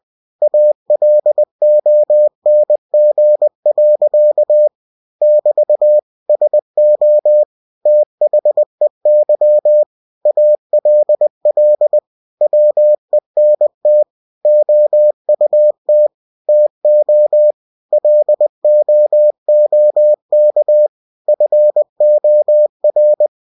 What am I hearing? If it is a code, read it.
Morse code, heard as a keyed tone, about 20 wpm, 600 Hz. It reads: ALONG. <BT> SO THEY ALL WENT OUT TO LOOK FOR
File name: cw-books / winnie-the-pooh / winnie-20wpm-page-0066